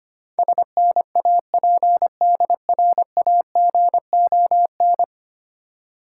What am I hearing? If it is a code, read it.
Morse code, heard as a keyed tone, about 25 wpm, 700 Hz. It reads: SNAPDRAGON
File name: practice / 25wpm / Difficult Words with Hints - 25wpm